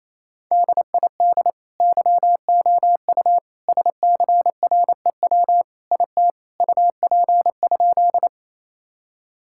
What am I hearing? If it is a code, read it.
Morse code, heard as a keyed tone, about 28 wpm, 700 Hz. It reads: DID YOU SCREW IT UP?
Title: Sentences from Top 1600 Words 28wpm